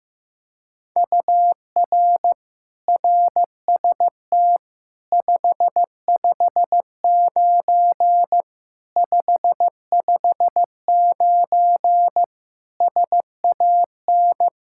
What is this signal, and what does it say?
Morse code, heard as a keyed tone, about 15 wpm, 700 Hz. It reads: UR RST 559 559 SAN
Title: Practice QSO - Exchange 1 - Beginning - 15wpm